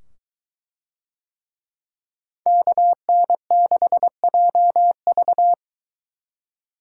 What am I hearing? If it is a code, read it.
Morse code, heard as a keyed tone, about 22 wpm, 700 Hz. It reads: KN6JV